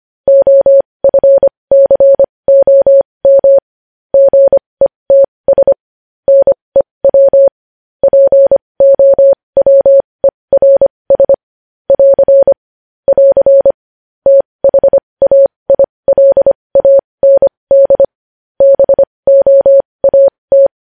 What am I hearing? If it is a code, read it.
Morse code, heard as a keyed tone, about 25 wpm, spaced as for 20 wpm, 550 Hz. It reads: OFCOM GETS NEW POWERS <AR> <AR> THAILAND BOAT
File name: podcast / News headlines-1x-25x20